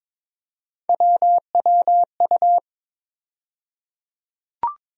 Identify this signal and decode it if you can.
Morse code, heard as a keyed tone, about 21 wpm, 700 Hz. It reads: WWU
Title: USA Call Sign Postfixes - 3 Letters 22wpm